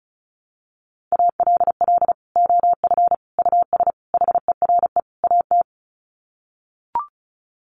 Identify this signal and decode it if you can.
Morse code, heard as a keyed tone, about 34 wpm, 700 Hz. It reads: ALL OF US HERE AT
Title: Sets of 5 Words 35wpm